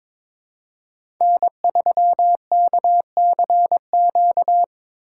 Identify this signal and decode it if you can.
Morse code, heard as a keyed tone, about 22 wpm, 700 Hz. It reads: N3KCQ
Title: USA Call Signs - 5 Characters - 2x Character Spacing - 22wpm